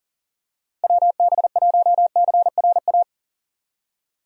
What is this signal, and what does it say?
Morse code, heard as a keyed tone, about 40 wpm, 700 Hz. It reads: WB1CRA
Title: USA Call Signs - 6 Characters - 40wpm